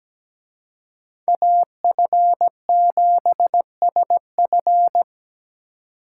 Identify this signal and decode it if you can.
Morse code, heard as a keyed tone, about 20 wpm, 700 Hz. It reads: AF7SF
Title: Speed Racing - USA 2x2 Call Signs 17wpm